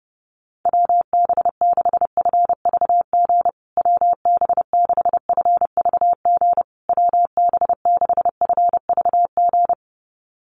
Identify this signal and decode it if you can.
Morse code, heard as a keyed tone, about 30 wpm, 700 Hz. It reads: WB6FVG WB6FVG WB6FVG
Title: USA Call Signs Repeated 3x 30wpm